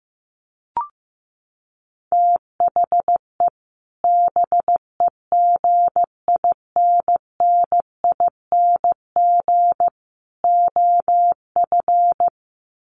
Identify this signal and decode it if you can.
Morse code, heard as a keyed tone, about 15 wpm, 700 Hz. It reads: THE BEGINNING OF
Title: Sets of 3 Words 15wpm